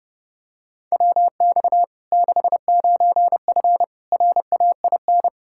Morse code, heard as a keyed tone, about 30 wpm, 700 Hz. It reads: WX 69F RAIN
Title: Practice QSO - Exchange 2 - Info - 30wpm